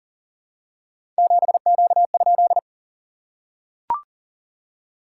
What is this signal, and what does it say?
Morse code, heard as a keyed tone, about 37 wpm, 690 Hz. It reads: 7Q?
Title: SOTA QSO Practice 40wpm